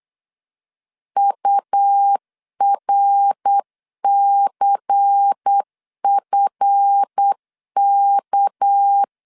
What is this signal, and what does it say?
Morse code, heard as a keyed tone, about 8 wpm, 800 Hz. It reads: URCFK
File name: LEZIONI / 7 GF9Z / 7 GF 8wpm 3108